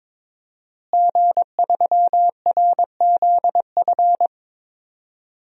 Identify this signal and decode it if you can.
Morse code, heard as a keyed tone, about 22 wpm, 700 Hz. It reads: G3RZF
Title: DXCC-100 Call Signs - 5 Characters - Repeated 2x - 22wpm